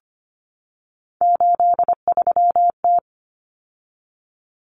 Morse code, heard as a keyed tone, about 25 wpm, 700 Hz. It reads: 83T
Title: Common Cut Numbers - 3-Digit 25wpm